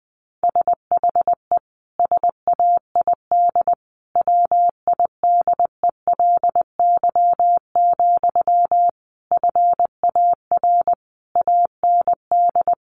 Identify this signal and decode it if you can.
Morse code, heard as a keyed tone, about 20 wpm, 700 Hz. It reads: SHE SAID WIDELY, FAR AND
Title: Sentences from Top 1200 Words 20wpm